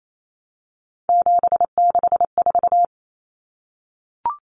Morse code, heard as a keyed tone, about 28 wpm, 700 Hz. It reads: 764
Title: Numbers - 3-Digit 28wpm